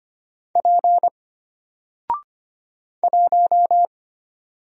Morse code, heard as a keyed tone, about 25 wpm, 700 Hz. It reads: P 1
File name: Single-Letter-Number-25wpm